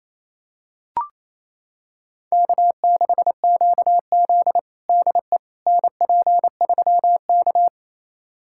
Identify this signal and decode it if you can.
Morse code, heard as a keyed tone, about 28 wpm, 700 Hz. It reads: K6QZ DE NP3K
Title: USA Callsign DE Callsign - 4 Characters - 28wpm